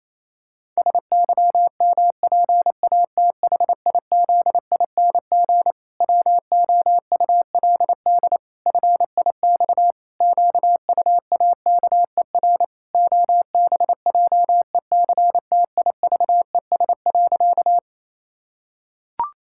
Morse code, heard as a keyed tone, about 27 wpm, 700 Hz. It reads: SYMPATHIZING WOULD FIX QUAKER OBJECTIVES.
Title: Pangram Challenge v2 - 28wpm